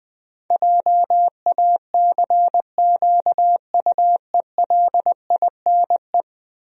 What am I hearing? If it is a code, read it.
Morse code, heard as a keyed tone, about 20 wpm, 700 Hz. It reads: JACQUELINE
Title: USA Common Names - 20wpm